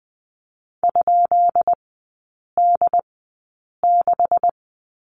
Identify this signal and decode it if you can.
Morse code, heard as a keyed tone, about 20 wpm, spaced as for 8 wpm, 700 Hz. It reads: ?D6
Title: 172 Lesson 28e - Number 9 - Three Characters 20wpm